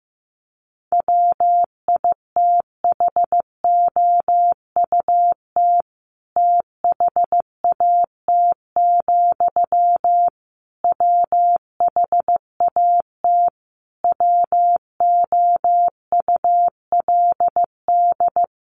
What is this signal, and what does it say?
Morse code, heard as a keyed tone, about 15 wpm, 700 Hz. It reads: WITHOUT THAT, WHAT WOULD
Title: Sentences from Top 600 Words 15wpm